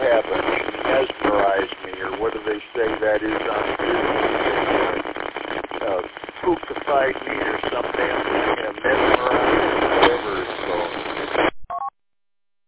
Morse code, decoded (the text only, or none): none